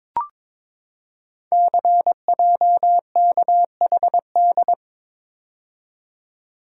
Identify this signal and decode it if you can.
Morse code, heard as a keyed tone, about 22 wpm, 700 Hz. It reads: CJKHD
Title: Code Groups - 5 Letter:Numbers - 22wpm